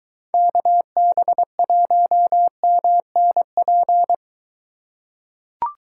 Morse code, heard as a keyed tone, about 23 wpm, 700 Hz. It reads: KB1MNP